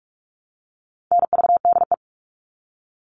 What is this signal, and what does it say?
Morse code, heard as a keyed tone, about 44 wpm, 700 Hz. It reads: NVDE
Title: All Cut Numbers - 4-Digit 45wpm